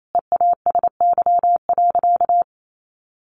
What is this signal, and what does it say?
Morse code, heard as a keyed tone, about 28 wpm, 700 Hz. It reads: EASY.
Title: Sentences from Top 700 Words 28wpm